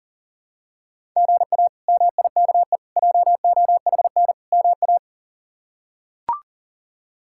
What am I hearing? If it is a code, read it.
Morse code, heard as a keyed tone, about 37 wpm, 700 Hz. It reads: GA MIKE JOHN MA
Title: K1USN SST QSO Practice - 40wpm